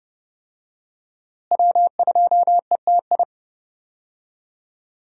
Morse code, heard as a keyed tone, about 32 wpm, 700 Hz. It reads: W2ETI
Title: USA Call Signs - 5 Characters - Repeated 2x - 30wpm